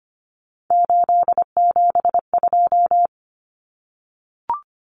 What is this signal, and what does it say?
Morse code, heard as a keyed tone, about 25 wpm, 700 Hz. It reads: 872
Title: Common Cut Numbers - 3-Digit 25wpm